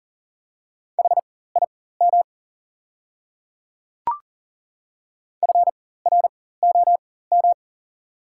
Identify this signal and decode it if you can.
Morse code, heard as a keyed tone, about 40 wpm, spaced as for 19 wpm, 700 Hz. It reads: HIM FROM
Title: 4x Character Spacing - Top 300 Words - 40wpm